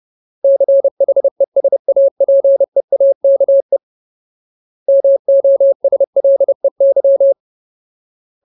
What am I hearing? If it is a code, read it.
Morse code, heard as a keyed tone, about 30 wpm, 550 Hz. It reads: CHESAPEAKE MOSLEY
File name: Ultimate Word List / 30WPM Ultimate